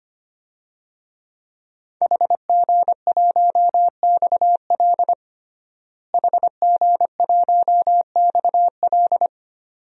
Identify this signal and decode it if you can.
Morse code, heard as a keyed tone, about 25 wpm, 700 Hz. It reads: HG1XL HG1XL
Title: DXCC-100 Call Signs - 5 Characters - Repeated 2x - 25wpm